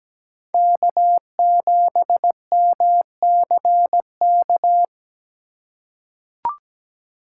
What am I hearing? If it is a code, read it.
Morse code, heard as a keyed tone, about 17 wpm, 700 Hz. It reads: K7MCK
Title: USA Call Signs - 5 Characters - 4x Character Spacing - 17wpm